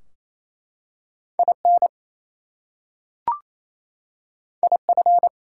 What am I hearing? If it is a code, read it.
Morse code, heard as a keyed tone, about 29 wpm, 700 Hz. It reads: IN IF
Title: Two Letter Words - 28wpm